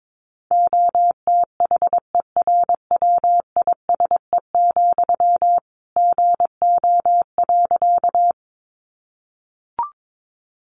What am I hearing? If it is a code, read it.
Morse code, heard as a keyed tone, about 22 wpm, 700 Hz. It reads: OTHERWISE, GO.
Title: Sentences from Top 1200 Words 22wpm